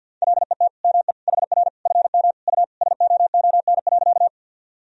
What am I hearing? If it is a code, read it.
Morse code, heard as a keyed tone, about 46 wpm, 700 Hz. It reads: LET ME HR FM U SOON.
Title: Sentences from Top 400 Words and Abbreviations 50wpm